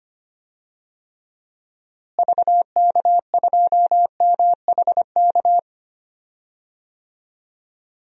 Morse code, heard as a keyed tone, about 25 wpm, 700 Hz. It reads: VK2MHK
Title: DXCC-100 Call Signs - 6 Characters - Repeated 2x - 25wpm